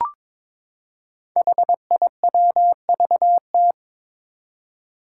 Morse code, heard as a keyed tone, about 22 wpm, 700 Hz. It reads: HIWVT
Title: Five Challenging Letter-Number Combinations 22wpm